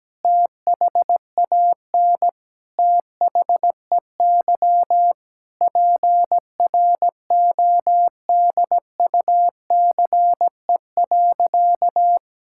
Morse code, heard as a keyed tone, about 17 wpm, 700 Hz. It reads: THAN THEY PRODUCE.
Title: Sentences from Top 1000 Words - Review Part 5 - 17wpm